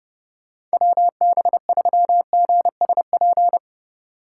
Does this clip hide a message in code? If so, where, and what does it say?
Yes: Morse code from 0.7 to 3.6 s, heard as a keyed tone, about 32 wpm, 700 Hz. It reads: WB3GSP